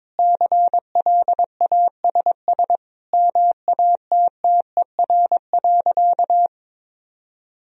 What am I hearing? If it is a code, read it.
Morse code, heard as a keyed tone, about 22 wpm, 700 Hz. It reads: CLASS MATTER.
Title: Sentences from Top 1000 Words - Review Part 2 - 22wpm